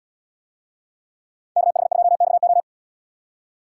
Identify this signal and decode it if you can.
Morse code, heard as a keyed tone, about 77 wpm, 700 Hz. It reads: KH2XO